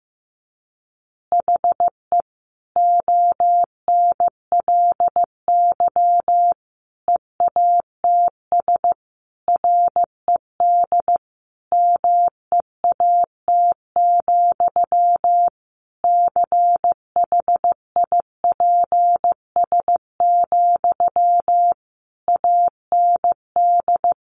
Morse code, heard as a keyed tone, about 15 wpm, 700 Hz. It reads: HE ONLY EATS RED MEAT, CHIPS, AND
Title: Easy Phrases 15wpm